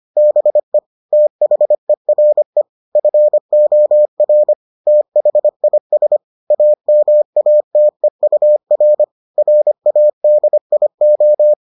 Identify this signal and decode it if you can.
Morse code, heard as a keyed tone, about 25 wpm, 600 Hz. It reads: BE THERE FOR THIS AMATEUR RADIO